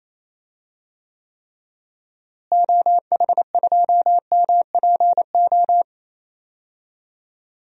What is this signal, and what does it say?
Morse code, heard as a keyed tone, about 28 wpm, 700 Hz. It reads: OH2MPO
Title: DXCC-100 Call Signs - 6 Characters - Repeated 3x - 28wpm